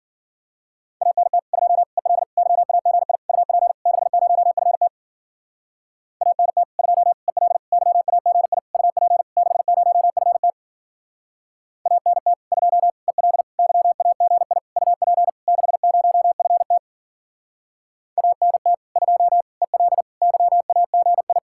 Morse code, heard as a keyed tone, about 51 wpm, 700 Hz. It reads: ANT 2 EL YAGI UP 60FT ANT 2 EL YAGI UP 60FT ANT 2 EL YAGI UP 60FT ANT 2 EL YAGI